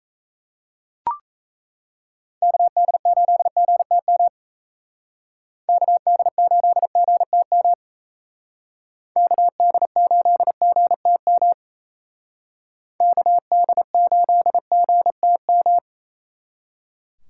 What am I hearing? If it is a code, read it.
Morse code, heard as a keyed tone, about 34 wpm, 700 Hz. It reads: KD8GTM KD8GTM KD8GTM KD8GTM
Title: Speed Racing - USA 2x3 Call Signs 28wpm